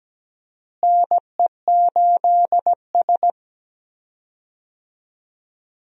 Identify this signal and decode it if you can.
Morse code, heard as a keyed tone, about 17 wpm, 700 Hz. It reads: NE8S
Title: USA Call Signs - 4 Characters - Repeated 3x - 17wpm